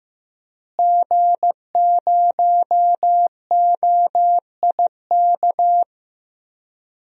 Morse code, heard as a keyed tone, about 15 wpm, 700 Hz. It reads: G0OIK
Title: DXCC-100 Call Signs - 5 Characters - Repeated 2x - 15wpm